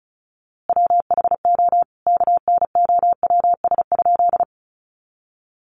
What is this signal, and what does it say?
Morse code, heard as a keyed tone, about 34 wpm, 700 Hz. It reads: WHO KNOWS?
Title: A Sound Of Thunder 35wpm